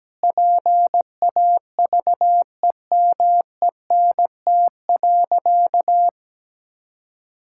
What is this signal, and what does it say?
Morse code, heard as a keyed tone, about 17 wpm, 700 Hz. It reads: PAVEMENT.
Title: Pangram Challenge v2 - 17wpm